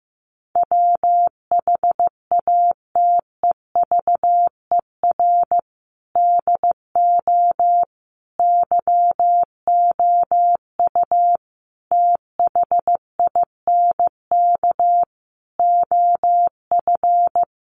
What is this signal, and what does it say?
Morse code, heard as a keyed tone, about 15 wpm, 700 Hz. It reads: WHATEVER DO YOU THINK OF
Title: Sentences from Top 1100 Words 15wpm